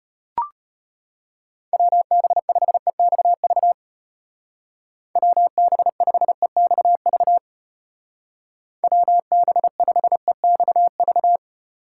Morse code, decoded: WB5EXV WB5EXV WB5EXV

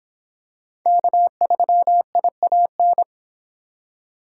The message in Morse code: K3IAN

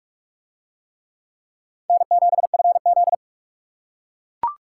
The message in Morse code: N7FZ

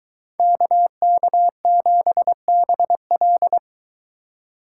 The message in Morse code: KK7BL

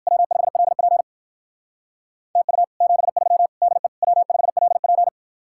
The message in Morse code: W5LP TU 73 DE W5LP

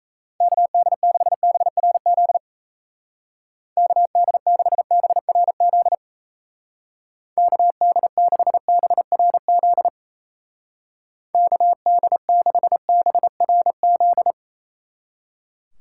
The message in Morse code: KD6BRZ KD6BRZ KD6BRZ KD6BRZ

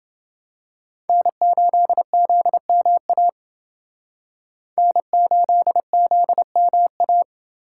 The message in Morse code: N8ZMA N8ZMA

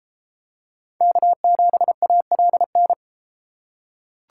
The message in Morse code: K7ALN

